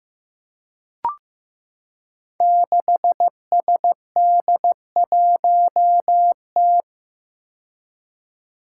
Morse code, heard as a keyed tone, about 15 wpm, 700 Hz. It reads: 6SD1T
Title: Five Challenging Letter-Number Combinations 15wpm